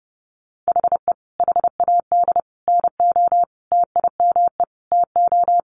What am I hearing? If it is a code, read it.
Morse code, heard as a keyed tone, about 30 wpm, 700 Hz. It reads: HE HAD NO TIME TO
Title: Sentences from Top 1000 Words 30wpm